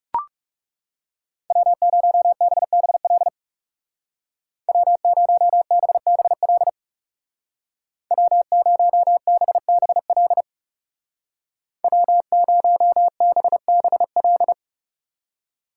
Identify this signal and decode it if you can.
Morse code, heard as a keyed tone, about 36 wpm, 700 Hz. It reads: W0BBL W0BBL W0BBL W0BBL
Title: Speed Racing - USA 1x3 Call Signs 30wpm